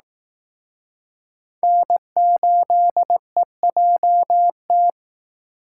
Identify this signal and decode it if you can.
Morse code, heard as a keyed tone, about 19 wpm, 700 Hz. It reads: N8EJT